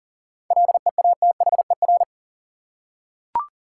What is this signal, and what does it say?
Morse code, heard as a keyed tone, about 40 wpm, 700 Hz. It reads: LEATHER
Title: Seven Letter Words - 40wpm